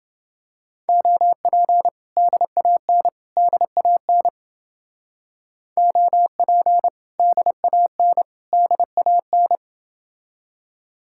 Morse code, heard as a keyed tone, about 31 wpm, 700 Hz. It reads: OP DAN DAN OP DAN DAN